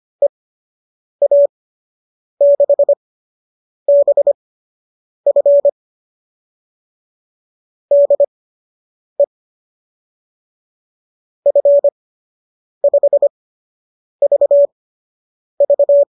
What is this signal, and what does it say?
Morse code, heard as a keyed tone, about 25 wpm, spaced as for 8 wpm, 590 Hz. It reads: EA6BF DE F5VV